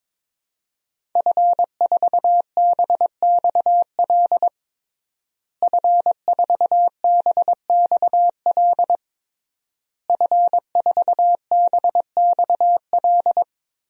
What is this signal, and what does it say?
Morse code, heard as a keyed tone, about 22 wpm, 700 Hz. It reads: F4BXL F4BXL F4BXL
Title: DXCC-100 Call Signs - 5 Characters - Repeated 3x - 22wpm